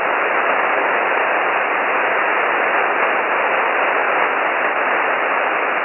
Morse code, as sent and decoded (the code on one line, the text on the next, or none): none